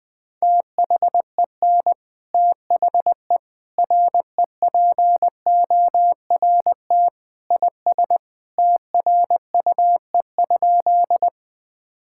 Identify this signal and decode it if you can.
Morse code, heard as a keyed tone, about 21 wpm, 700 Hz. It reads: THEN THE REPORT IS TRUE?
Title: Sentences from Top 1000 Words - Review Part 6 - 20wpm